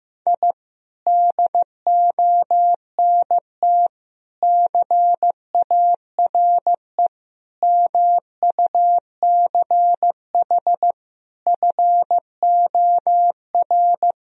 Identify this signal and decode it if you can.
Morse code, heard as a keyed tone, about 15 wpm, 700 Hz. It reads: I DONT CARE MUCH FOR
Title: Sentences from Top 1000 Words - Review Part 4 - 15wpm